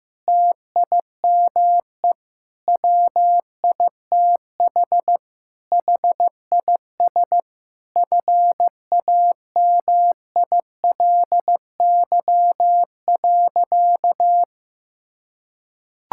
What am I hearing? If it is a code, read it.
Morse code, heard as a keyed tone, about 15 wpm, 700 Hz. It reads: TIME WITH HIS FAMILY.
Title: Sentences from Top 1000 Words - Review Part 6 - 15wpm